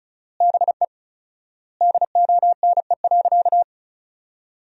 Morse code, heard as a keyed tone, about 36 wpm, 700 Hz. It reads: BE DONE.
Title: 4x Word-Spacing - Sentences from Top 500 Words - 35wpm